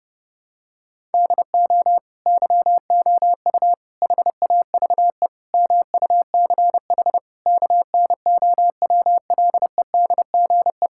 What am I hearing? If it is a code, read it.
Morse code, heard as a keyed tone, about 32 wpm, 700 Hz. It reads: DO YOU HAVE MUCH KNOWLEDGE
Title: Sentences from Top 1000 Words - Review Part 6 - 30wpm